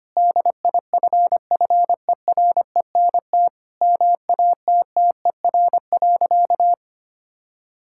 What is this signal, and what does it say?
Morse code, heard as a keyed tone, about 25 wpm, 700 Hz. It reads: DIFFERENT MATTER.
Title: Sentences from Top 1600 Words 25wpm